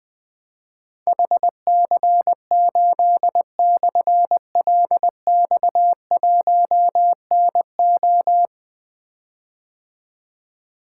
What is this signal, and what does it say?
Morse code, heard as a keyed tone, about 20 wpm, 700 Hz. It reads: HC8/LX1NO
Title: Difficult Call Signs 20wpm